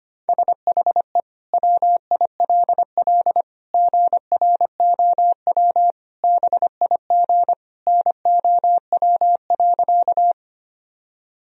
Morse code, heard as a keyed tone, about 25 wpm, 700 Hz. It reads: SHE WILL GROW BIG NOW.
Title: Sentences from Top 1000 Words - Review Part 1 - 25wpm